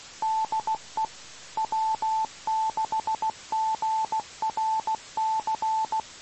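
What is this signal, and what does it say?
Morse code, heard as a keyed tone, about 16 wpm, 880 Hz. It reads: DE W6GRC
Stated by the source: 2 m band, 147.105 MHz.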